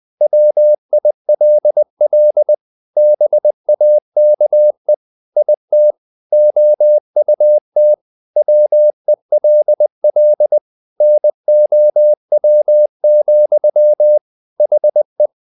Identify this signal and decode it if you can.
Morse code, heard as a keyed tone, about 20 wpm, 600 Hz. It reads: WILL BAKE IT OUT WELL NOW, HE
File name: the old man and the sea part 2 20wpm